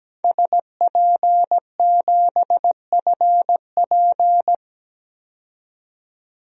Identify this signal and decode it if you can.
Morse code, heard as a keyed tone, about 17 wpm, 700 Hz. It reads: SP7FP
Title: DXCC-100 Call Signs - 5 Characters - 17wpm